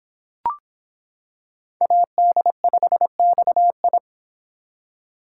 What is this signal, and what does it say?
Morse code, heard as a keyed tone, about 26 wpm, 700 Hz. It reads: AD5XI